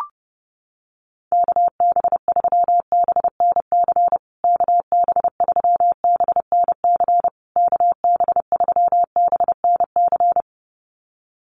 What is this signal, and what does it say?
Morse code, heard as a keyed tone, about 30 wpm, 700 Hz. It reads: KB3BNC KB3BNC KB3BNC
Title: USA Call Signs Repeated 3x 30wpm